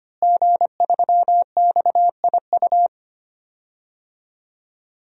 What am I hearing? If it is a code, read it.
Morse code, heard as a keyed tone, about 25 wpm, 700 Hz. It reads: G3XIU
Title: DXCC-100 Call Signs - 5 Characters - Repeated 2x - 25wpm